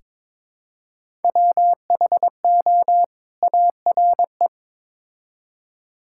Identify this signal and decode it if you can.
Morse code, heard as a keyed tone, about 22 wpm, 700 Hz. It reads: WHO ARE
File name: Build A Phrase - 2-Word Phrases - 22wpm